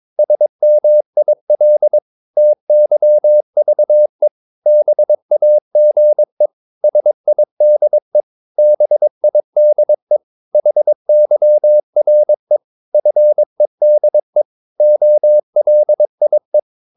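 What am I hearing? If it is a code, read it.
Morse code, heard as a keyed tone, about 22 wpm, 600 Hz. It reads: SMIL TYVE BAGE SIDE BIDE HYRE FEDE OLIE
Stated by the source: Morse practice of short Danish words at 22 wpm, from CW med Gnister.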